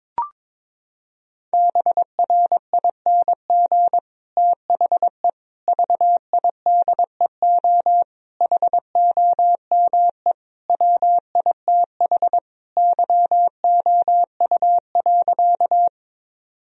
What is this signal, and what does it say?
Morse code, heard as a keyed tone, about 21 wpm, 700 Hz. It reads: BRING THE VIDEO HOME WITH YOU.
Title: Sentences from Top 1000 Words - Review Part 4 - 22wpm